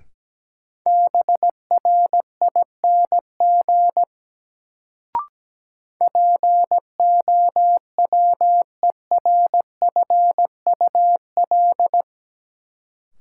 Morse code, heard as a keyed tone, about 17 wpm, 700 Hz. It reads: BRING POWERFUL
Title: Sets of 5 Words Encore - New Word Drill - 17wpm